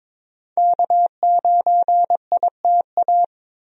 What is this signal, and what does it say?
Morse code, heard as a keyed tone, about 22 wpm, 700 Hz. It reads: K9ITA
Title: USA Call Signs - 5 Characters - Repeated 2x - 22wpm